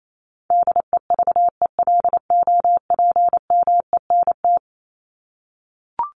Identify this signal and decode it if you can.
Morse code, heard as a keyed tone, about 28 wpm, 700 Hz. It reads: DEVELOPMENT
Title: Top 501-750 Words 28wpm